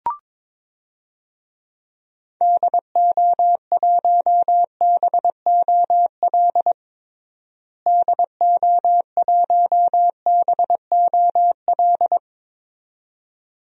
Morse code, heard as a keyed tone, about 23 wpm, 700 Hz. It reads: DO1BOL DO1BOL
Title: DXCC-100 Call Signs - 6 Characters - Repeated 2x - 22wpm